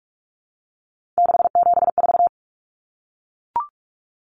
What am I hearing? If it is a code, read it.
Morse code, heard as a keyed tone, about 44 wpm, 700 Hz. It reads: 674
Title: Common Cut Numbers - 3-Digit 45wpm